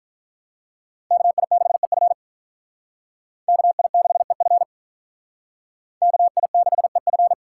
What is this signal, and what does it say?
Morse code, heard as a keyed tone, about 45 wpm, 700 Hz. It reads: KI6EF KI6EF KI6EF